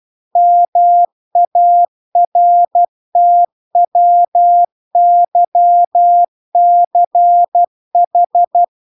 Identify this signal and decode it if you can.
Morse code, heard as a keyed tone, about 12 wpm, 700 Hz. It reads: MARTWYCH